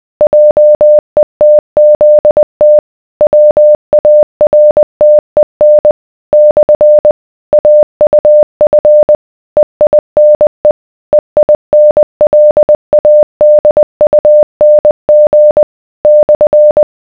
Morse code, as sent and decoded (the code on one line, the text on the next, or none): .--- . - --.. - / .-- .- .-. - . -. / -..-. / .- ..- ..-. / . .. -. / . .. -. .-.. .- -.. ..- -. --. / -..-.
JETZT WARTEN / AUF EINE EINLADUNG /